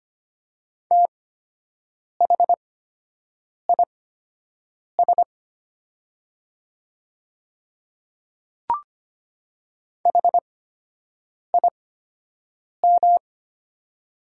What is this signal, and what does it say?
Morse code, heard as a keyed tone, about 25 wpm, spaced as for 7 wpm, 700 Hz. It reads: THIS HIM
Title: 8x Character Spacing - Top 100 Words - 25wpm